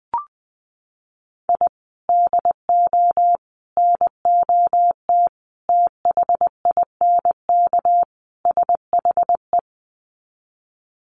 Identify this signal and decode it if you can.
Morse code, heard as a keyed tone, about 20 wpm, 700 Hz. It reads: I DO NOT THINK SHE
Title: Sets of 5 Words 20wpm